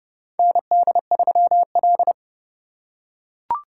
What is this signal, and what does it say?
Morse code, heard as a keyed tone, about 32 wpm, 700 Hz. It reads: ND3L
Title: K1USN SST QSO Practice - 30wpm